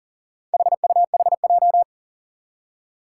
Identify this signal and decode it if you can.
Morse code, heard as a keyed tone, about 42 wpm, 700 Hz. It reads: HUHJ
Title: Four Challenging Letter-Number Combinations 40wpm